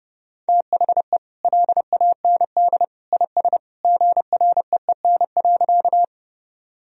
Morse code, heard as a keyed tone, about 30 wpm, 700 Hz. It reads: THE LAND IS GREEN.
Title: Sentences from Top 1000 Words - Review Part 2 - 30wpm